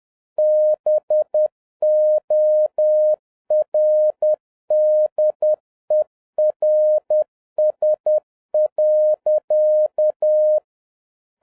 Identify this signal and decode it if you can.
Morse code, heard as a keyed tone, about 10 wpm, 610 Hz. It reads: BORDERS.